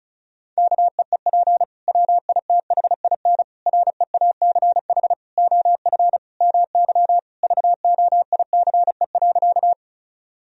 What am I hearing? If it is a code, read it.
Morse code, heard as a keyed tone, about 34 wpm, 700 Hz. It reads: KEEP WITHIN REACH OF MY VOICE.
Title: Sentences from Top 1000 Words - Review Part 4 - 35wpm